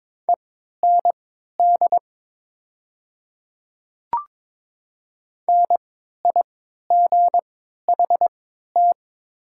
Morse code, heard as a keyed tone, about 21 wpm, spaced as for 12 wpm, 700 Hz. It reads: END NIGHT
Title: 3x Character Spacing - Top 300 Words - 22wpm